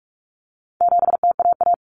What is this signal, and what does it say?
Morse code, heard as a keyed tone, about 44 wpm, 700 Hz. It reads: 7TAA
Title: Common Cut Numbers - 4-Digit 45wpm